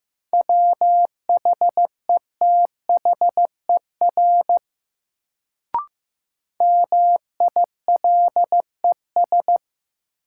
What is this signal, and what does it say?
Morse code, heard as a keyed tone, about 15 wpm, 700 Hz. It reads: WHETHER MILES
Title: Sentences from Top 1000 Words - Review Part 3 - New Word Drill - 15wpm